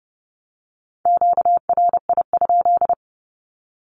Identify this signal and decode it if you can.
Morse code, heard as a keyed tone, about 30 wpm, 700 Hz. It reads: QRI?
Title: Q-Codes Repeated 3x 30wpm